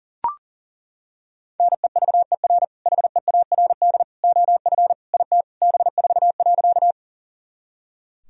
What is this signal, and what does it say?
Morse code, heard as a keyed tone, about 37 wpm, 700 Hz. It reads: NEVER HEARD OF IT B4.
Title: Sentences from Top 400 Words and Abbreviations 40wpm